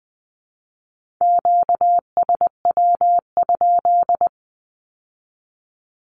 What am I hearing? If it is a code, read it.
Morse code, heard as a keyed tone, about 20 wpm, 700 Hz. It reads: QSW?
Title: Q-Codes with Spoken Meaning - 20wpm